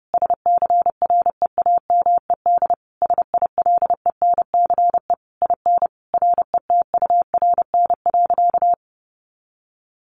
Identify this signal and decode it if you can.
Morse code, heard as a keyed tone, about 30 wpm, 700 Hz. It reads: SCREAMED SILENCE IN RETURN.
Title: A Sound Of Thunder 30wpm